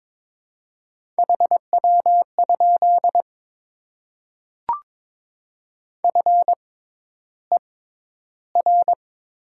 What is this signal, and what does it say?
Morse code, heard as a keyed tone, about 22 wpm, spaced as for 11 wpm, 700 Hz. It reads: HW? FER
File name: Top 20 QSO Elements - 6x Character Spacing - 22wpm